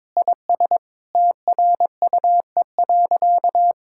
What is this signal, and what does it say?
Morse code, heard as a keyed tone, about 21 wpm, 700 Hz. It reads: IS TRUE.
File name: practice / 22wpm / Sentences with 8-Word Phrases - 22wpm